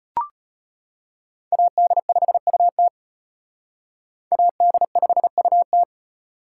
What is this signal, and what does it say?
Morse code, heard as a keyed tone, about 32 wpm, 700 Hz. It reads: AD5UT AD5UT